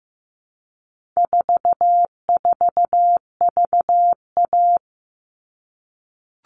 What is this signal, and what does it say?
Morse code, heard as a keyed tone, about 15 wpm, 700 Hz. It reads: 44VA
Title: All Cut Numbers - 4-Digit 15wpm